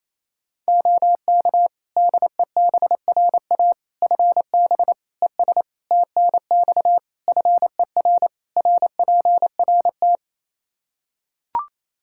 Morse code, heard as a keyed tone, about 28 wpm, 700 Hz. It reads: OK DEBRA FB ES TNX FER RPRT
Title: Practice QSO - Exchange 2 - Info - 28wpm